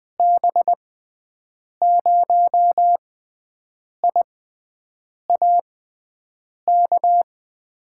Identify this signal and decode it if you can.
Morse code, heard as a keyed tone, about 20 wpm, spaced as for 7 wpm, 700 Hz. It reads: B0IAK
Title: USA Call Signs - 6 Characters - 6x Character Spacing - 20wpm